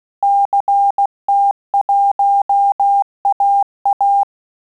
Morse code, heard as a keyed tone, about 16 wpm, 800 Hz. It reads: CT1AA
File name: QSO CW 75 CPM